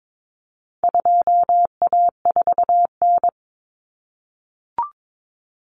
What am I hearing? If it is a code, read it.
Morse code, heard as a keyed tone, about 22 wpm, 700 Hz. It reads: 2A4N
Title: Common Cut Numbers - 4-Digit 22wpm